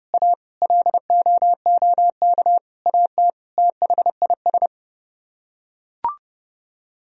Morse code, heard as a keyed tone, about 30 wpm, 700 Hz. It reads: A LOOK AT THIS